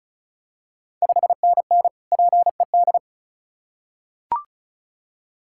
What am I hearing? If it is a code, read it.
Morse code, heard as a keyed tone, about 36 wpm, 700 Hz. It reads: 5NN PED
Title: Ontario QSO Party Exchange 35wpm